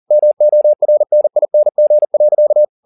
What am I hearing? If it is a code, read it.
Morse code, heard as a keyed tone, about 40 wpm, 600 Hz. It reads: MORNING.
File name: the old man and the sea 2 40wpm